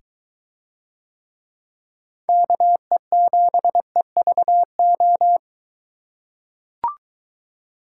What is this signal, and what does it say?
Morse code, heard as a keyed tone, about 22 wpm, 700 Hz. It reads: KE7EVO